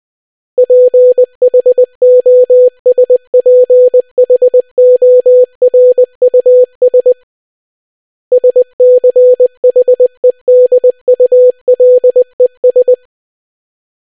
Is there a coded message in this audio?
Yes: Morse code from 0.6 to 12.9 s, heard as a keyed tone, about 20 wpm, 500 Hz. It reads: PHOSPHORUS SCHEDULES